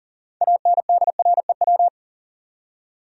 Morse code, heard as a keyed tone, about 40 wpm, 700 Hz. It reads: ANDREW
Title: USA Common Names - 40wpm